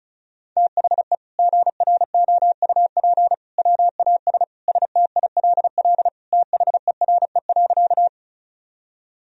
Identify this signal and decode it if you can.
Morse code, heard as a keyed tone, about 36 wpm, 700 Hz. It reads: THE GROUP WAS STILL THERE.
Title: Sentences from Top 1000 Words - Review Part 1 - 35wpm